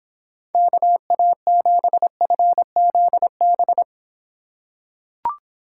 Morse code, heard as a keyed tone, about 26 wpm, 700 Hz. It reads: KA7FZB